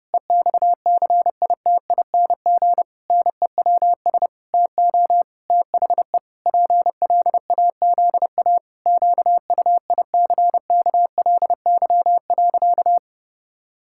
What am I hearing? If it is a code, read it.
Morse code, heard as a keyed tone, about 30 wpm, 700 Hz. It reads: EXCITING NEWS TO THE PLAZA QUICKLY.
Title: Pangram Challenge v2 - 30wpm